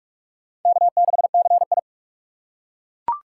Morse code, heard as a keyed tone, about 42 wpm, 700 Hz. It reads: K6CI